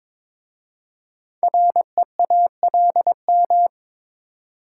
Morse code, heard as a keyed tone, about 22 wpm, 700 Hz. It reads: REALM
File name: 5 Letter Words - First 3 Letters Given - 22wpm